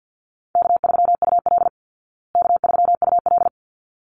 Morse code, heard as a keyed tone, about 50 wpm, 700 Hz. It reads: K3UL K3UL
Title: Easy Call Signs Repeated 2x - 50wpm